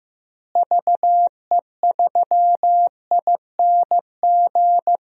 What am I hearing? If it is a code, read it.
Morse code, heard as a keyed tone, about 15 wpm, 700 Hz. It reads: VE3ING
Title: DXCC-100 Call Signs - 6 Characters - Repeated 3x - 15wpm